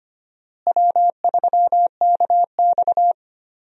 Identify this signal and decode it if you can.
Morse code, heard as a keyed tone, about 25 wpm, 700 Hz. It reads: W3KX
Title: USA Call Signs - 4 Characters - Repeated 2x - 25wpm